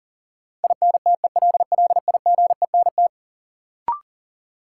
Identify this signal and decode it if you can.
Morse code, heard as a keyed tone, about 37 wpm, 700 Hz. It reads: INTELLIGENT
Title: Sentences from Top 1000 Words - Review Part 7 - New Word Drill - 40wpm